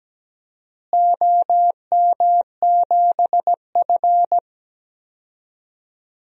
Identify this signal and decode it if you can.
Morse code, heard as a keyed tone, about 17 wpm, 700 Hz. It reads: OM7F
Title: DXCC-100 Call Signs - 4 Characters - Repeated 2x - 17wpm